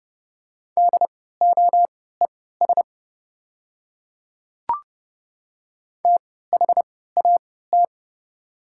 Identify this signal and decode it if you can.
Morse code, heard as a keyed tone, about 30 wpm, spaced as for 17 wpm, 690 Hz. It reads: DOES THAT
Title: 3x Character Spacing - Top 200 Words - 30wpm